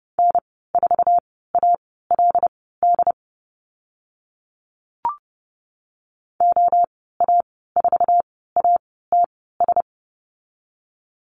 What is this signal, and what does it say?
Morse code, heard as a keyed tone, about 30 wpm, spaced as for 17 wpm, 700 Hz. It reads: N4ALD OA4ATS
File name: Lesson 14g - Number 2 - Call Signs 30wpm